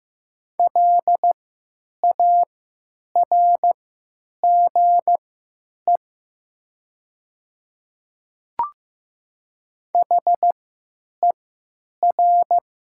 Morse code, heard as a keyed tone, about 15 wpm, spaced as for 9 wpm, 700 Hz. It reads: LARGE HER
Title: 3x Character Spacing - Top 200 Words - 15wpm